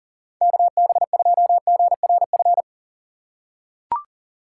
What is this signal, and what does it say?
Morse code, heard as a keyed tone, about 40 wpm, 700 Hz. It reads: KB2GRF
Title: USA Call Signs - 6 Characters - 2x Character Spacing - 40wpm